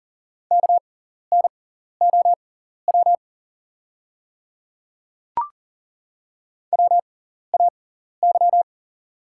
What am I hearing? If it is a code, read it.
Morse code, heard as a keyed tone, about 38 wpm, spaced as for 14 wpm, 690 Hz. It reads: KNOW WAY
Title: 6x Character Spacing - Top 100 Words - 40wpm